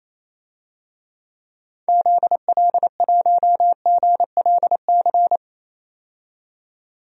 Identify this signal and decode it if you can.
Morse code, heard as a keyed tone, about 28 wpm, 700 Hz. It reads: ZL1GLC
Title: DXCC-100 Call Signs - 6 Characters - 28wpm